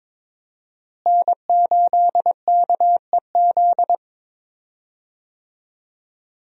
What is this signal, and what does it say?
Morse code, heard as a keyed tone, about 22 wpm, 700 Hz. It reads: N8KEZ
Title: USA Call Signs - 5 Characters - Repeated 2x - 22wpm